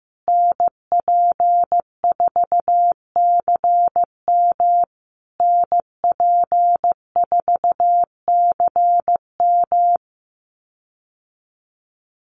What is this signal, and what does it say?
Morse code, heard as a keyed tone, about 15 wpm, 700 Hz. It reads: NP4CM NP4CM
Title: USA Call Signs Repeated 3x 15wpm